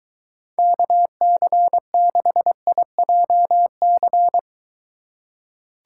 Morse code, heard as a keyed tone, about 21 wpm, 700 Hz. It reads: KC6IJC